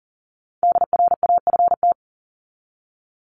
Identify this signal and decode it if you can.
Morse code, heard as a keyed tone, about 40 wpm, 700 Hz. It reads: DRAFT
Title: Top 1751-2000 Words 40wpm